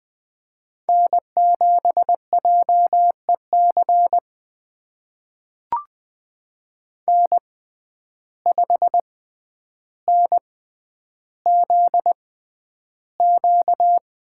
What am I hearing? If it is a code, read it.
Morse code, heard as a keyed tone, about 20 wpm, spaced as for 10 wpm, 700 Hz. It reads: N7JEC N5NZQ
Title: USA Call Signs - 5 Characters - 6x Character Spacing - 20wpm